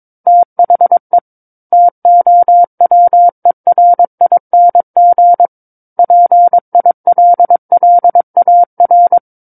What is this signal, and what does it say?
Morse code, heard as a keyed tone, about 22 wpm, 700 Hz. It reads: THE TOWERING PILLAR
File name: Book One / 13-Chapter 13 - 22 WPM